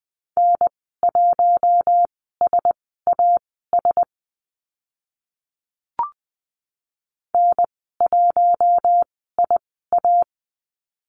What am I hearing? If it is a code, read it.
Morse code, heard as a keyed tone, about 20 wpm, spaced as for 15 wpm, 700 Hz. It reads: N1SAS N1IA
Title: 042 Lesson 8g - Number 1 - Call Signs 20wpm